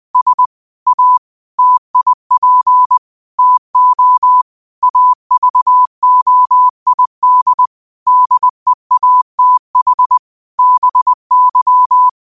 Morse code, heard as a keyed tone, about 20 wpm, 1000 Hz. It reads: S A TIP TO AVOID DEATH BY
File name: qotdfiles / 20wpm